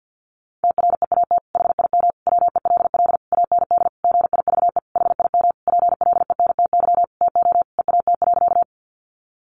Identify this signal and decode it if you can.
Morse code, heard as a keyed tone, about 50 wpm, 700 Hz. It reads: TREAT HIM WELL AND GIVE HIM PLENTY TO EAT.
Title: Sentences from Top 1300 Words 50wpm